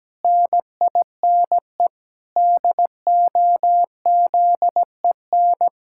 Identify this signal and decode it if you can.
Morse code, heard as a keyed tone, about 17 wpm, 700 Hz. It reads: NINE DOZEN
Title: Pangram Challenge v2 - 17wpm